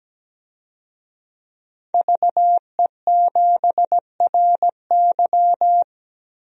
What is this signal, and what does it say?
Morse code, heard as a keyed tone, about 17 wpm, 700 Hz. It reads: VE7RY
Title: DXCC-100 Call Signs - 5 Characters - 17wpm